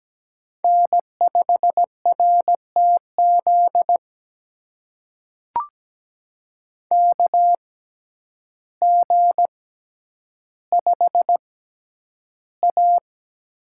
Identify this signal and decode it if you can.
Morse code, heard as a keyed tone, about 17 wpm, spaced as for 9 wpm, 700 Hz. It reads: N5RTZ KG5A
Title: USA Call Signs - 5 Characters - 6x Character Spacing - 17wpm